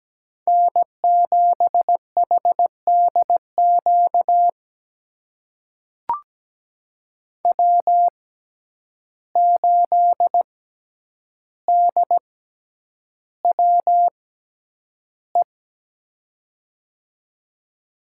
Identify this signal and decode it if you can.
Morse code, heard as a keyed tone, about 17 wpm, spaced as for 9 wpm, 700 Hz. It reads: N7HDQ W8DWE
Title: USA Call Signs - 5 Characters - 6x Character Spacing - 17wpm